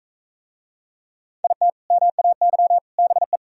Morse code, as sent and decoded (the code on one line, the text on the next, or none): .. - / -- .- -.-- / -... .
IT MAY BE